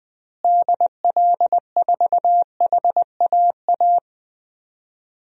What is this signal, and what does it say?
Morse code, heard as a keyed tone, about 20 wpm, 700 Hz. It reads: DL4HAA
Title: DXCC-100 Call Signs - 6 Characters - Repeated 3x - 20wpm